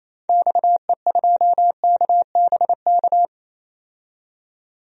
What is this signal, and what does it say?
Morse code, heard as a keyed tone, about 28 wpm, 700 Hz. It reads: XE2KBK
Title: DXCC-100 Call Signs - 6 Characters - 28wpm